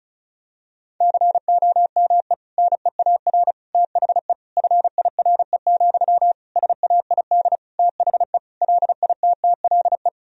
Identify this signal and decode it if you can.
Morse code, heard as a keyed tone, about 36 wpm, 700 Hz. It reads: COME NEAR THE FIRE, SAID THE LITTLE
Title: Sentences from Top 1000 Words - Review Part 1 - 35wpm